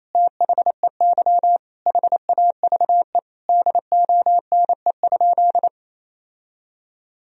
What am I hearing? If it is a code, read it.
Morse code, heard as a keyed tone, about 28 wpm, 700 Hz. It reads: THEY HAVE DONE?
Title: Sentences from Top 1000 Words - Review Part 3 - 28wpm